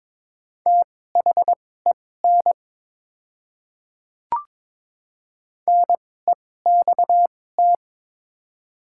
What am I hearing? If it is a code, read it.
Morse code, heard as a keyed tone, about 22 wpm, spaced as for 16 wpm, 700 Hz. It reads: THEN NEXT
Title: 2x Character Spacing - Top 300 Words - 22wpm